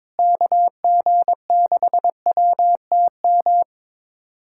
Morse code, heard as a keyed tone, about 21 wpm, 700 Hz. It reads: KG6WTM
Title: USA Callsign DE Callsign 22wpm